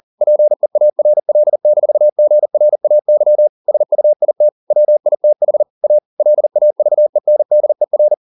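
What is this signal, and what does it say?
Morse code, heard as a keyed tone, about 40 wpm, 600 Hz. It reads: PEARL-GRAY SUIT WITH A LAVENDER
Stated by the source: Morse code at 40 wpm, practice text, CW med Gnister.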